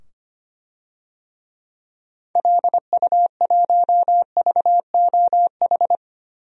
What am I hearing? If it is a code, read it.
Morse code, heard as a keyed tone, about 25 wpm, 700 Hz. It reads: LU1VOH
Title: DXCC-100 Call Signs - 6 Characters - 25wpm